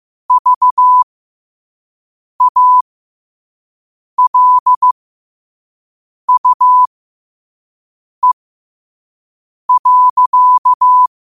Morse code, heard as a keyed tone, about 15 wpm, spaced as for 5 wpm, 1000 Hz. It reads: VALUE.
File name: 5wpm